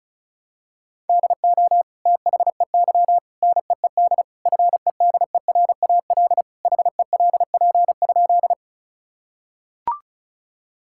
Morse code, heard as a keyed tone, about 36 wpm, 700 Hz. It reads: DO THEY NEED FEDERAL HELP?
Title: Sentences from Top 1000 Words - Review Part 5 - 35wpm